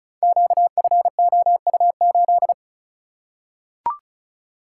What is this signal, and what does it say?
Morse code, heard as a keyed tone, about 36 wpm, 700 Hz. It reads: QFOU8
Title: Code Groups - 5 Letter:Numbers - 35wpm